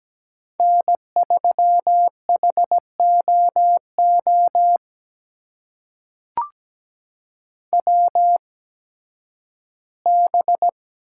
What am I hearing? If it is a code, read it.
Morse code, heard as a keyed tone, about 17 wpm, spaced as for 11 wpm, 700 Hz. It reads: N3HOO WB
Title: USA Call Signs - 8x Character Spacing - 17wpm